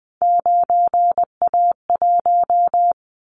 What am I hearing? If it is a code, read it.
Morse code, heard as a keyed tone, about 20 wpm, 700 Hz. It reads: 9A1
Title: Callsign Prefixes for Top 100 DXCC - 20wpm